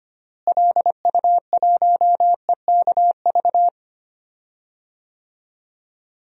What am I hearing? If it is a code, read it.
Morse code, heard as a keyed tone, about 25 wpm, 700 Hz. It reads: LU1EKV
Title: DXCC-100 Call Signs - 6 Characters - Repeated 3x - 25wpm